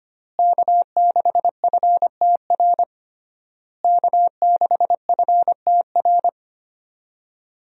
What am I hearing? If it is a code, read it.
Morse code, heard as a keyed tone, about 25 wpm, 700 Hz. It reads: K6FTR K6FTR
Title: USA Call Signs - 5 Characters - Repeated 3x - 25wpm